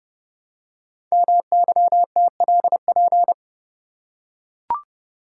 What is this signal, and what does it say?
Morse code, heard as a keyed tone, about 30 wpm, 700 Hz. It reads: MYTLP
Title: Code Groups - 5 Letters - 30wpm